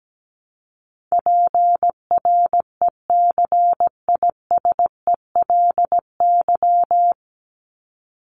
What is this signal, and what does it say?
Morse code, heard as a keyed tone, about 17 wpm, 700 Hz. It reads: PRECISELY
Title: LY Suffix - Top 100 Words with Base Words 17wpm